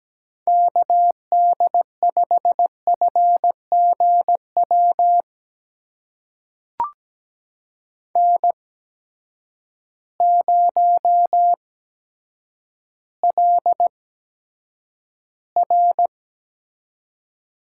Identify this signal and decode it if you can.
Morse code, heard as a keyed tone, about 17 wpm, spaced as for 9 wpm, 700 Hz. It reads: KD5FGW N0LR
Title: USA Call Signs - 8x Character Spacing - 17wpm